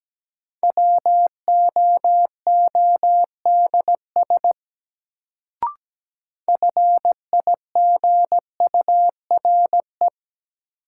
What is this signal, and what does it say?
Morse code, heard as a keyed tone, about 17 wpm, 700 Hz. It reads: WOODS FIGURE
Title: Sentences from Top 1000 Words - Review Part 2 - New Word Drill - 17wpm